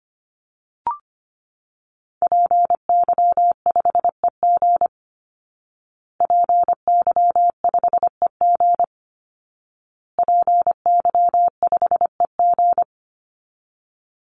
Morse code, heard as a keyed tone, about 25 wpm, 700 Hz. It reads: PY5EG PY5EG PY5EG
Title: Call Signs Repeated 2x - 25wpm